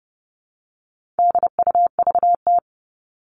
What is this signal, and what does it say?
Morse code, heard as a keyed tone, about 30 wpm, 700 Hz. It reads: DUVT
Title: All Cut Numbers - 4-Digit 30wpm